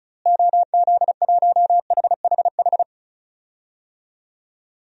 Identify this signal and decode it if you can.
Morse code, heard as a keyed tone, about 36 wpm, 700 Hz. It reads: OZ1HHH
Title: DXCC-100 Call Signs - 6 Characters - Repeated 2x - 35wpm